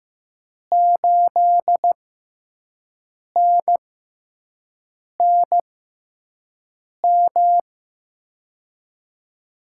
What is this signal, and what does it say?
Morse code, heard as a keyed tone, about 15 wpm, spaced as for 5 wpm, 700 Hz. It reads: 8NNM
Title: USA Call Signs - 6x Character Spacing - 15wpm